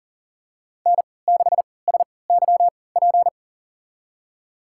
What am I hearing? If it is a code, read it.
Morse code, heard as a keyed tone, about 40 wpm, spaced as for 23 wpm, 700 Hz. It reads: N6SYP
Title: USA Call Signs - 3x Character Spacing - 40wpm